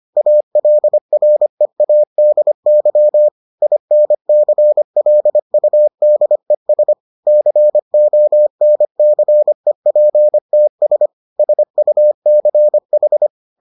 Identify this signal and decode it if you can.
Morse code, heard as a keyed tone, about 25 wpm, 600 Hz. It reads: ALREADY INCLUDES CONCEPTS SUCH